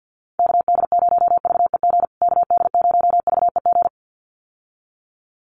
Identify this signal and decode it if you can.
Morse code, heard as a keyed tone, about 50 wpm, 700 Hz. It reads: KD0VEG KD0VEG
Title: USA Call Signs Repeated 2x 50wpm